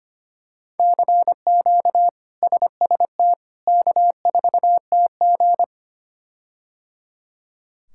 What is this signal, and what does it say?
Morse code, heard as a keyed tone, about 25 wpm, 700 Hz. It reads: CQ SST K4TG
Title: K1USN SST QSO Practice - 25wpm